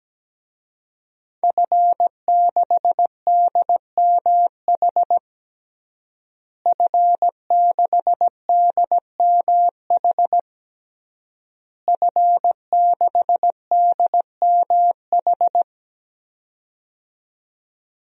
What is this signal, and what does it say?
Morse code, heard as a keyed tone, about 17 wpm, 700 Hz. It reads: F6DMH F6DMH F6DMH
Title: DXCC-100 Call Signs - 5 Characters - Repeated 3x - 17wpm